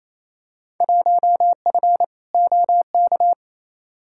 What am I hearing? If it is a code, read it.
Morse code, heard as a keyed tone, about 28 wpm, 700 Hz. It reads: 1F OK
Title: Field Day Exchanges Encore - 28wpm